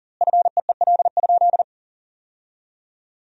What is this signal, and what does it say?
Morse code, heard as a keyed tone, about 40 wpm, 700 Hz. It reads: FEEL?
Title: 2x Word-Spacing - Sentences from Top 400 Words - 40wpm